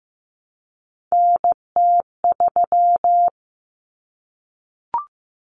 Morse code, heard as a keyed tone, about 15 wpm, 700 Hz. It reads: NT3
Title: Common Cut Numbers - 3-Digit 15wpm